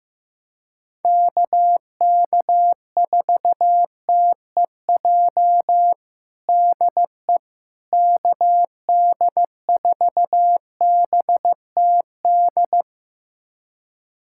Morse code, heard as a keyed tone, about 15 wpm, 700 Hz. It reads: KK4TEJ DE KD4BTD
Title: USA Callsign DE Callsign - 6 Characters - 15wpm